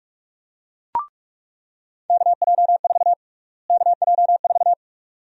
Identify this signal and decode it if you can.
Morse code, heard as a keyed tone, about 46 wpm, 700 Hz. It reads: KJ4 KJ4
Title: USA Call Sign Prefixes - Repeated 2x - 45wpm